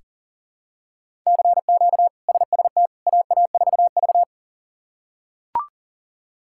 Morse code, heard as a keyed tone, about 40 wpm, 700 Hz. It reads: CQ SST AA4V
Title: K1USN SST QSO Practice - 40wpm